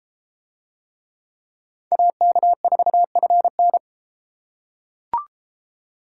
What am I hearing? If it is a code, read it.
Morse code, heard as a keyed tone, about 31 wpm, 700 Hz. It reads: AK4FN